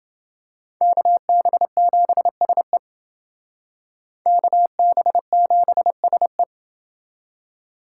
Morse code, heard as a keyed tone, about 28 wpm, 700 Hz. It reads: KB7SE KB7SE